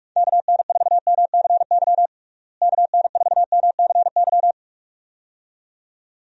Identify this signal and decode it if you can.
Morse code, heard as a keyed tone, about 41 wpm, 700 Hz. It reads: KN4MCY KN4MCY
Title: USA Call Signs - 6 Characters - Repeated 2x - 45wpm